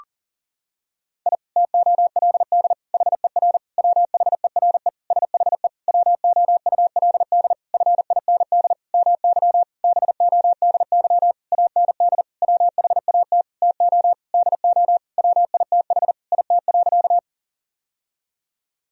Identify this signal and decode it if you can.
Morse code, heard as a keyed tone, about 40 wpm, 700 Hz. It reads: I TOLD HER WHERE SHE WOULD FIND MY BODY AND WHAT TO DO WITH IT.
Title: Sentences from Top 1000 Words - Review Part 2 - 40wpm